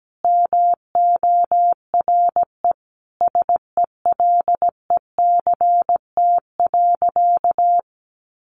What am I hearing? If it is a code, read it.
Morse code, heard as a keyed tone, about 17 wpm, 700 Hz. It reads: MORE SELECT.
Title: Sentences from Top 1300 Words 17wpm